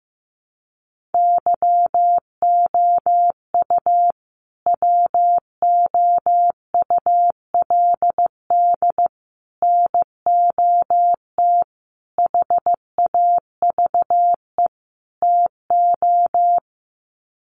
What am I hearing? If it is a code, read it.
Morse code, heard as a keyed tone, about 15 wpm, 700 Hz. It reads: YOU WOULD NOT HAVE TO
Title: Sets of 5 Words Encore 15wpm